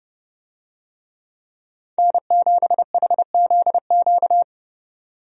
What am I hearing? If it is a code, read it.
Morse code, heard as a keyed tone, about 29 wpm, 700 Hz. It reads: N7HZQ
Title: USA Call Signs - 5 Characters - 30wpm